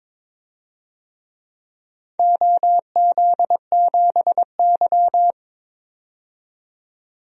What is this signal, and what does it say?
Morse code, heard as a keyed tone, about 21 wpm, 700 Hz. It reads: OZ7Y
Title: DXCC-100 Call Signs - 4 Characters - 22wpm